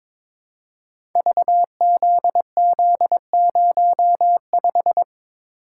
Morse code, heard as a keyed tone, about 22 wpm, 700 Hz. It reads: VZZ05
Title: Code Groups - 5 Letter:Numbers - 22wpm